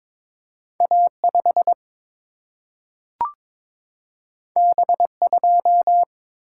Morse code, heard as a keyed tone, about 22 wpm, 700 Hz. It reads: A 5 B 2
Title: Single-Letter Single-Number 22wpm